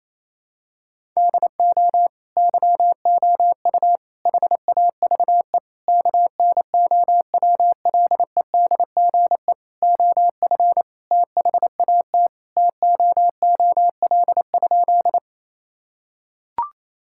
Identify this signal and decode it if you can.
Morse code, heard as a keyed tone, about 27 wpm, 700 Hz. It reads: DO YOU HAVE KNOWLEDGE OF THAT TOOL?
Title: Sentences from Top 1000 Words - Review Part 6 - 28wpm